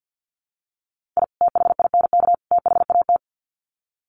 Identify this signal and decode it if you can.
Morse code, heard as a keyed tone, about 50 wpm, 700 Hz. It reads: I THINK THAT
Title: Sets of 3 Words 50wpm